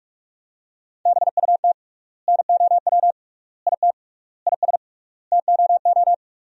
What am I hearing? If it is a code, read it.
Morse code, heard as a keyed tone, about 46 wpm, 700 Hz. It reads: BUT NOW IT IS TOO
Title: 3x Word-Spacing - Sentences from Top 300 Words - 45wpm